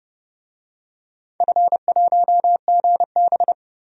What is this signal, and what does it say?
Morse code, heard as a keyed tone, about 30 wpm, 700 Hz. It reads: F1GB
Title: DXCC-100 Call Signs - 4 Characters - 30wpm